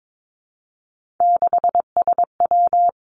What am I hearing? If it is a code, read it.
Morse code, heard as a keyed tone, about 22 wpm, 700 Hz. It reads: 6SW
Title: Three Challenging Letter-Number Combinations 22wpm